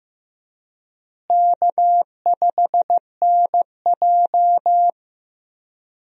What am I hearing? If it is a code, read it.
Morse code, heard as a keyed tone, about 15 wpm, 700 Hz. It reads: K5NJ
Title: USA Call Signs - 4 Characters - Repeated 3x - 15wpm